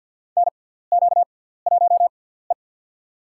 Morse code, heard as a keyed tone, about 48 wpm, spaced as for 17 wpm, 700 Hz. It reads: NQ1E
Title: USA Call Signs - 6x Character Spacing - 50wpm